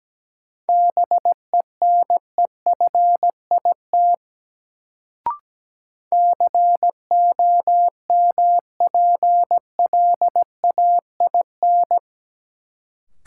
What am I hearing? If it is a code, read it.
Morse code, heard as a keyed tone, about 17 wpm, 700 Hz. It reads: BENEFIT COMPLAIN
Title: Sentences from Top 1300 Words - New Word Drill Alt - 17wpm